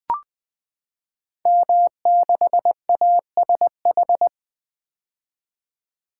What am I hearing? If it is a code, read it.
Morse code, heard as a keyed tone, about 20 wpm, 700 Hz. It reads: M6ASH
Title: Code Groups - 5 Letter:Numbers - 20wpm